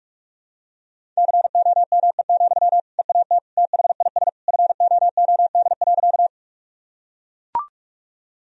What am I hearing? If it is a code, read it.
Morse code, heard as a keyed tone, about 43 wpm, 700 Hz. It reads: COME, EAT THIS FOOD.
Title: Sentences from Top 1000 Words - Review Part 1 - 45wpm